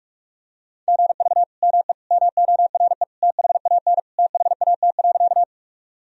Morse code, heard as a keyed tone, about 44 wpm, 700 Hz. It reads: GV ME MORE THAN THAT.
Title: Sentences from Top 200 Words and Abbreviations 45wpm